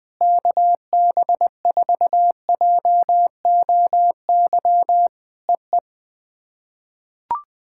Morse code, heard as a keyed tone, about 20 wpm, 700 Hz. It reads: KB4JOY EE
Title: POTA QSO Practice - 20wpm